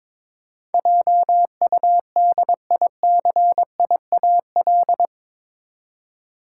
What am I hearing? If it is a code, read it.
Morse code, heard as a keyed tone, about 22 wpm, 700 Hz. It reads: JUDICIAL
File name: practice / 22wpm / Eight Letter Words - 22wpm